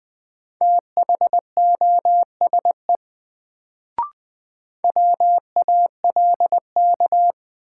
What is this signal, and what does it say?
Morse code, heard as a keyed tone, about 20 wpm, 700 Hz. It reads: THOSE WALK
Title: Sentences from Top 300 Words - New Word Drill - 20wpm